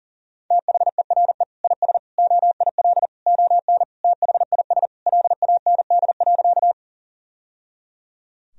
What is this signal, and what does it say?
Morse code, heard as a keyed tone, about 40 wpm, 700 Hz. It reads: THERE IS OIL ON THIS LAND.
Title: Sentences from Top 1000 Words - Review Part 4 - 40wpm